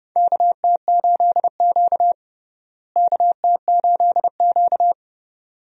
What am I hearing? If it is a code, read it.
Morse code, heard as a keyed tone, about 31 wpm, 700 Hz. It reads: KT8Q KT8Q
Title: USA Call Signs - 4 Characters - Repeated 2x - 30wpm